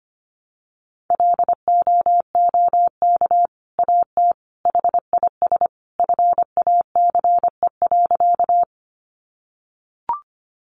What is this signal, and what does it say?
Morse code, heard as a keyed tone, about 25 wpm, 700 Hz. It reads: LOOK AT HIS FACE.
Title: Sentences from Top 300 Words 25wpm